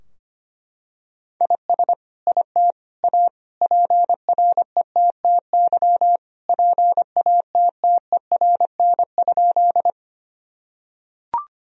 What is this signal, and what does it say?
Morse code, heard as a keyed tone, about 25 wpm, 700 Hz. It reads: IS IT A PRETTY PATTERN?
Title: Sentences from Top 1000 Words - Review Part 2 - 25wpm